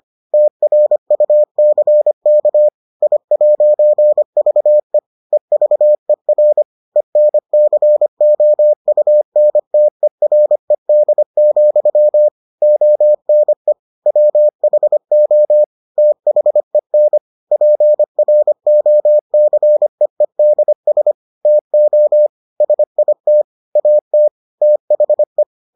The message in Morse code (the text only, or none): TRUCK I'VE EVER ENCOUNTERED, ONE WHO THEN PROCEEDS TO SIT AT THE